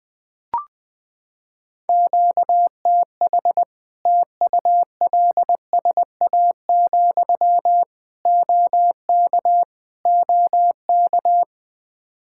Morse code, heard as a keyed tone, about 19 wpm, 700 Hz. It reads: QTH TULSA, OK OK
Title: Practice QSO - Exchange 1 - Beginning - 20wpm